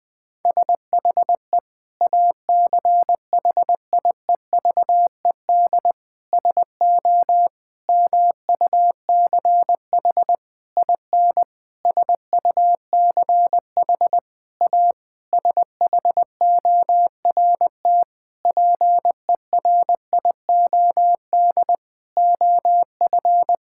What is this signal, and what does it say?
Morse code, heard as a keyed tone, about 20 wpm, 700 Hz. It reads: SHE ACHIEVED SO MUCH IN SUCH A SHORT PERIOD OF
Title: Sentences with 7-Word Phrases - 20wpm